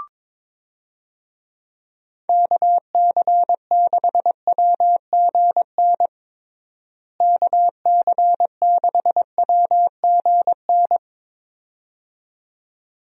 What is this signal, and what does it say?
Morse code, heard as a keyed tone, about 22 wpm, 700 Hz. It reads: KC6WGN KC6WGN
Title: USA Call Signs - 6 Characters - Repeated 2x - 22wpm